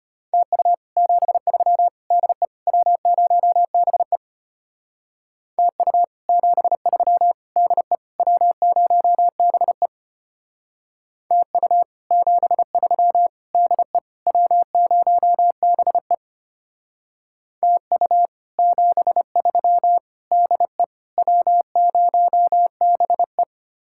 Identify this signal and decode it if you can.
Morse code, heard as a keyed tone, about 32 wpm, 700 Hz. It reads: TU 73 DE W0BE TU 73 DE W0BE TU 73 DE W0BE TU 73 DE W0BE